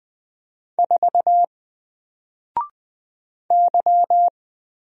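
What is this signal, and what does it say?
Morse code, heard as a keyed tone, about 20 wpm, 700 Hz. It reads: 4 Y
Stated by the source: Single letter and number Morse drill at 20 wpm, from N4LCD.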